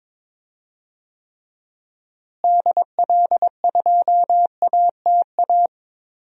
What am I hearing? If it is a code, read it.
Morse code, heard as a keyed tone, about 22 wpm, 700 Hz. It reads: DL2ATA
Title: DXCC-100 Call Signs - 6 Characters - Repeated 2x - 22wpm